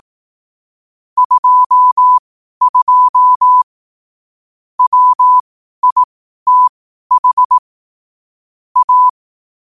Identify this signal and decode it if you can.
Morse code, heard as a keyed tone, about 18 wpm, spaced as for 12 wpm, 1000 Hz. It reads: 22 WITH A
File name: qotdfiles / 12wpm